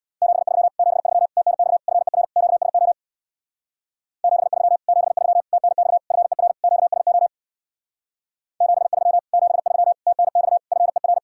Morse code, heard as a keyed tone, about 62 wpm, 700 Hz. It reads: 73 73 TNX FER QSO 73 73 TNX FER QSO 73 73 TNX FER
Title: Speed-Racing Common QSO Phrases - 50wpm